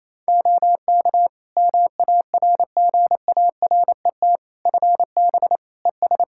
Morse code, heard as a keyed tone, about 28 wpm, 700 Hz. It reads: OK MARGARET FB ES
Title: Practice QSO - Exchange 2 - Info - 28wpm